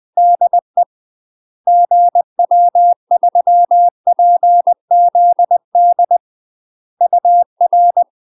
Morse code, heard as a keyed tone, about 20 wpm, 700 Hz. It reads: DE GW3PZD UR
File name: UK MASTER QSOs TRAINING / 20WPM / UK MASTER QSOs 0038